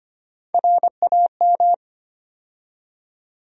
Morse code, heard as a keyed tone, about 25 wpm, 700 Hz. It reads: RAM
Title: USA Call Sign Postfixes - 3 Letters 25wpm